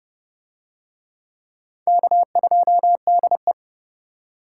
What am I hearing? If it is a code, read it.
Morse code, heard as a keyed tone, about 28 wpm, 700 Hz. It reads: K2DE